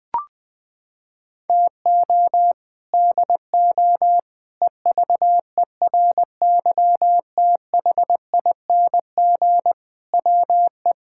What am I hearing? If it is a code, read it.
Morse code, heard as a keyed tone, about 20 wpm, 700 Hz. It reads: TO DO EVERYTHING WE
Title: Sets of 6 Words - 20wpm